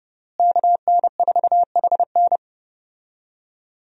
Morse code, heard as a keyed tone, about 30 wpm, 700 Hz. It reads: KN4HN
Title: USA Call Signs - 5 Characters - Repeated 3x - 30wpm